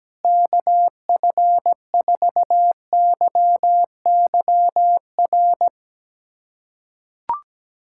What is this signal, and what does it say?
Morse code, heard as a keyed tone, about 17 wpm, 700 Hz. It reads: KF4YYR
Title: USA Callsign DE Callsign 17wpm